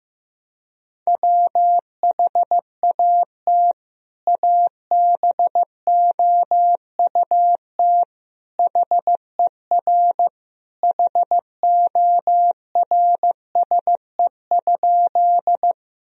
Morse code, heard as a keyed tone, about 15 wpm, 700 Hz. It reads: WHAT ABOUT HER HORSE?
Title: Sentences from Top 1000 Words - Review Part 1 - 15wpm